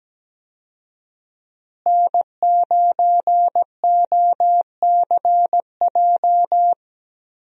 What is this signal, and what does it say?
Morse code, heard as a keyed tone, about 17 wpm, 700 Hz. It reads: N9OCJ
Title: USA Call Signs - 5 Characters - Repeated 3x - 17wpm